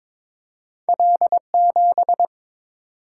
Morse code, heard as a keyed tone, about 22 wpm, 700 Hz. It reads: L7
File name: practice / 22wpm / Two Letter-Number Combinations - 22wpm